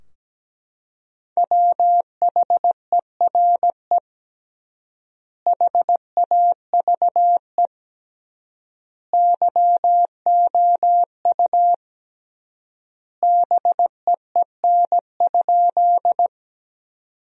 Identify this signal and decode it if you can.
Morse code, heard as a keyed tone, about 17 wpm, 700 Hz. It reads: WHERE HAVE YOU BEEN?
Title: 3x Word-Spacing - Sentences from Top 200 Words - 17wpm